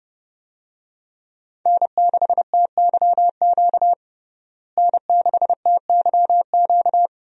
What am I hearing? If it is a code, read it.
Morse code, heard as a keyed tone, about 30 wpm, 700 Hz. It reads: N6TYQ N6TYQ
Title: USA Call Signs - 5 Characters - Repeated 2x - 30wpm